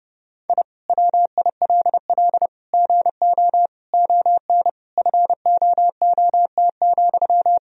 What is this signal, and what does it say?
Morse code, heard as a keyed tone, about 30 wpm, 700 Hz. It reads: I WILL GO ON FOOT,
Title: Sentences from Top 1000 Words - Review Part 3 - 30wpm